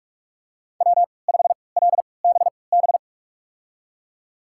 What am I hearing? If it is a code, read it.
Morse code, heard as a keyed tone, about 46 wpm, spaced as for 25 wpm, 700 Hz. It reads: W5LBB
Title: USA Call Signs - 5 Characters - 3x Character Spacing - 45wpm